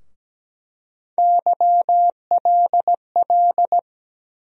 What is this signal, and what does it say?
Morse code, heard as a keyed tone, about 17 wpm, 700 Hz. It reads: YLL
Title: USA Call Sign Postfixes - 3 Letters 17wpm